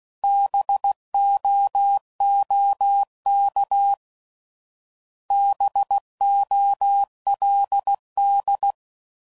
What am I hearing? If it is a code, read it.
Morse code, heard as a keyed tone, about 16 wpm, 800 Hz. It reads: BOOK BOLD